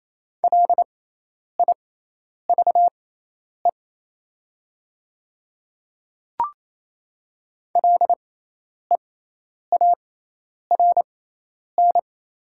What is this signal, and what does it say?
Morse code, heard as a keyed tone, about 27 wpm, spaced as for 10 wpm, 700 Hz. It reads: LIVE LEARN
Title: 6x Character Spacing - Top 300 Words - 28wpm